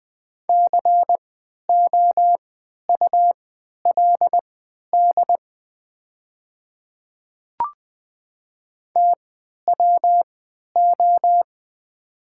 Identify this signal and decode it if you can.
Morse code, heard as a keyed tone, about 20 wpm, spaced as for 11 wpm, 690 Hz. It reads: COULD TWO
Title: 3x Character Spacing - Top 200 Words - 20wpm